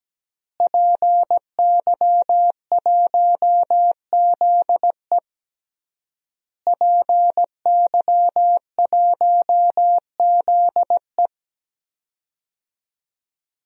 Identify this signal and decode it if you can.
Morse code, heard as a keyed tone, about 17 wpm, 700 Hz. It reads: PY1ZE PY1ZE
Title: DXCC-100 Call Signs - 5 Characters - Repeated 3x - 17wpm